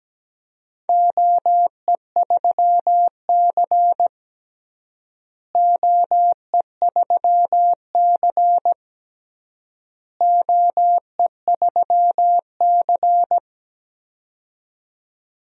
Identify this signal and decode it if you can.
Morse code, heard as a keyed tone, about 17 wpm, 700 Hz. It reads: OE3C OE3C OE3C
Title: DXCC-100 Call Signs - 4 Characters - Repeated 3x - 17wpm